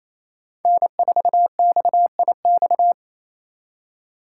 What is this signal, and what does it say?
Morse code, heard as a keyed tone, about 28 wpm, 700 Hz. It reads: N4XIX
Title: USA Call Signs - 2x Character Spacing - 28wpm